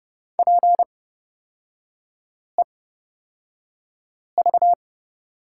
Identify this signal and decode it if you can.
Morse code, heard as a keyed tone, about 29 wpm, spaced as for 5 wpm, 700 Hz. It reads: PEV
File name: Lesson 36e - BT - Three Characters 30wpm